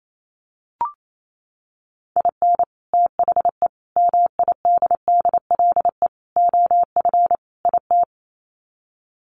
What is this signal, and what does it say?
Morse code, heard as a keyed tone, about 28 wpm, 700 Hz. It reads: IN THE MIDDLE OF IT
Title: Sets of 5 Words Encore 28wpm